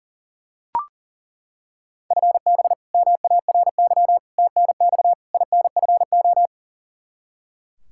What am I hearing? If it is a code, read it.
Morse code, heard as a keyed tone, about 40 wpm, 700 Hz. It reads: FB MARY TNX INFO
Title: Practice QSO - Exchange 3 - More Info - 40wpm